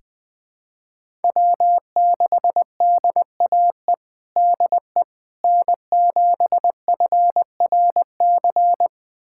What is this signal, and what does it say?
Morse code, heard as a keyed tone, about 21 wpm, 700 Hz. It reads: W6DAE DE N7FRC
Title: USA Callsign DE Callsign - 5 Characters - 20wpm